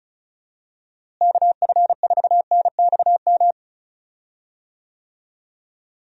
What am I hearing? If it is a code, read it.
Morse code, heard as a keyed tone, about 34 wpm, 700 Hz. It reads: KF4NXM
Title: USA Call Signs - 6 Characters - Repeated 2x - 35wpm